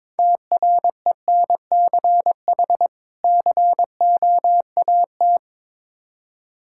Morse code, heard as a keyed tone, about 21 wpm, 700 Hz. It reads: TRENCH COAT
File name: Difficult Words with Hints - 22wpm